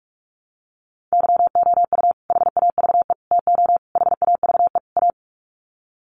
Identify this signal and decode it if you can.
Morse code, heard as a keyed tone, about 44 wpm, 700 Hz. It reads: YOU HAVE TO HAVE A
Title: Sets of 5 Words Encore 45wpm